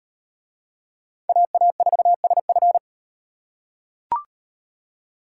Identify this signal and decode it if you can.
Morse code, heard as a keyed tone, about 38 wpm, 700 Hz. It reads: AA4SF